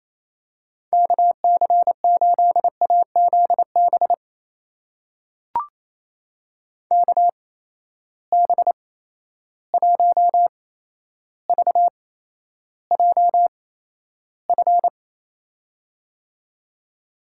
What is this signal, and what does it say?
Morse code, heard as a keyed tone, about 27 wpm, spaced as for 12 wpm, 700 Hz. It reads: KC8AZB KB1VJF
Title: USA Call Signs - 6 Characters - 8x Character Spacing - 28wpm